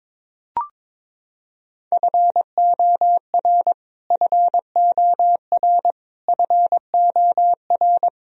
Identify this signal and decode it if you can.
Morse code, heard as a keyed tone, about 22 wpm, 700 Hz. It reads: FOR FOR FOR
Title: Top 100 Words Repeated 22wpm